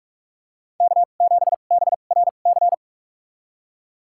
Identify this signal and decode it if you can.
Morse code, heard as a keyed tone, about 45 wpm, spaced as for 32 wpm, 700 Hz. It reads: K7BRC
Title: USA Call Signs - 5 Characters - 2x Character Spacing - 45wpm